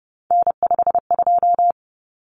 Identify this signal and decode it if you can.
Morse code, heard as a keyed tone, about 30 wpm, 700 Hz. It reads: N52
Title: Common Cut Numbers - 3-Digit 30wpm